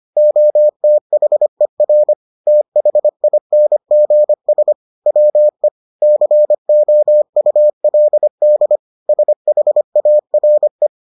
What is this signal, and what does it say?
Morse code, heard as a keyed tone, about 25 wpm, 600 Hz. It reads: OTHER THINGS WE COULD SHARE